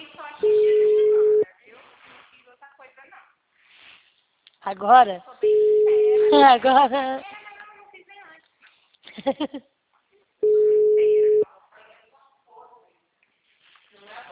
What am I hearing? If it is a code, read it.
Ringback tone (Europe, 425 Hz), with speech.